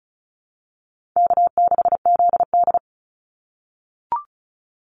Morse code, heard as a keyed tone, about 34 wpm, 700 Hz. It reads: K6ZD
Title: Call Signs - 35wpm